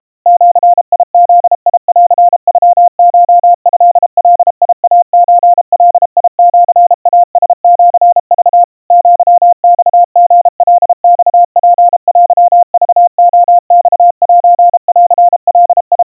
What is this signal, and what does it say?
Morse code (Japanese Wabun code), heard as a keyed tone, about 32 wpm, 700 Hz. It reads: ジブンノコトガイソガシイラシク アマリカマツテクレマセンガ
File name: kawa / Morse sample / kamyu 120 cpm 700 Hz Crispy